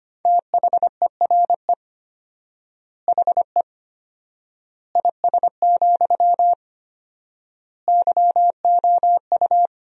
Morse code, heard as a keyed tone, about 25 wpm, 700 Hz. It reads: THERE HE IS, YOU
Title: 4x Word-Spacing - Sentences from Top 100 Words - 25wpm